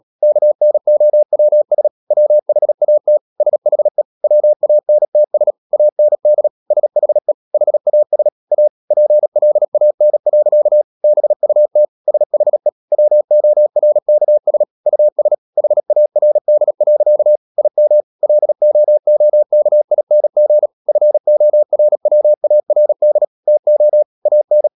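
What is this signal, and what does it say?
Morse code, heard as a keyed tone, about 37 wpm, 600 Hz. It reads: KNOWS WHAT SHE WANTS AND SHE HAS A PLAN. BUT SHE WORKS US HARD. IM LOOKING FORWARD TO AN
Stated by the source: Morse practice at 36 wpm, text of Ken Follett's novel Never, from CW med Gnister.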